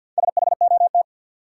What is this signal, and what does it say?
Morse code, heard as a keyed tone, about 53 wpm, 700 Hz. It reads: SHOT
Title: Sentences from Top 1100 Words - New Word Drill - 50wpm